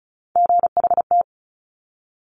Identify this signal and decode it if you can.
Morse code, heard as a keyed tone, about 36 wpm, 700 Hz. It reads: GHT
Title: Top Three Letter Combinations in Words 35wpm